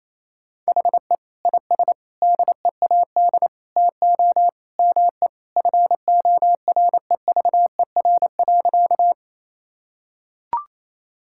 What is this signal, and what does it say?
Morse code, heard as a keyed tone, about 28 wpm, 700 Hz. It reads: HE IS DEAD TO ME FOREVER.
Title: Sentences from Top 1600 Words 28wpm